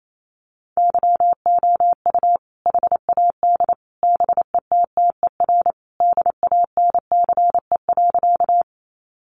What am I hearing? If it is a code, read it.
Morse code, heard as a keyed tone, about 28 wpm, 700 Hz. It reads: YOU HAD BETTER DANCE.
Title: Sentences from Top 700 Words 28wpm